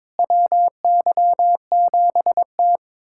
Morse code, heard as a keyed tone, about 22 wpm, 700 Hz. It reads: WY7T
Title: USA Callsign DE Callsign - 4 Characters - 22wpm